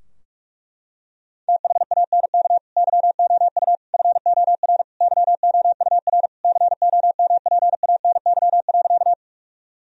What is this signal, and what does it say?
Morse code, heard as a keyed tone, about 46 wpm, 700 Hz. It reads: THANK YOU FOR YOUR COMPANY.
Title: Sentences from Top 1000 Words - Review Part 3 - 45wpm